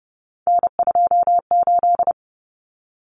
Morse code, heard as a keyed tone, about 30 wpm, 700 Hz. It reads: N28
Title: Common Cut Numbers - 3-Digit 30wpm